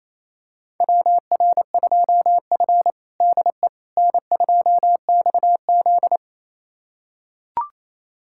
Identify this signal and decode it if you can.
Morse code, heard as a keyed tone, about 27 wpm, 700 Hz. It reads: WR2F DE N2XZ
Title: USA Callsign DE Callsign - 4 Characters - 28wpm